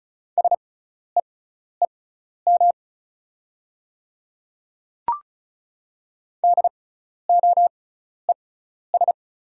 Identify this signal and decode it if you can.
Morse code, heard as a keyed tone, about 36 wpm, spaced as for 12 wpm, 700 Hz. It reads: SEEM DOES
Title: 6x Character Spacing - Top 300 Words - 35wpm